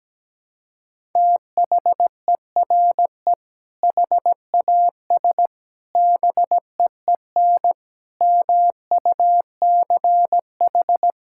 Morse code, heard as a keyed tone, about 17 wpm, 700 Hz. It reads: THERE HAS BEEN MUCH
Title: Sentences from Top 1000 Words - Review Part 5 - 17wpm